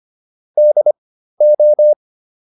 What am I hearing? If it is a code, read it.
Morse code, heard as a keyed tone, about 25 wpm, spaced as for 13 wpm, 600 Hz. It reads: DO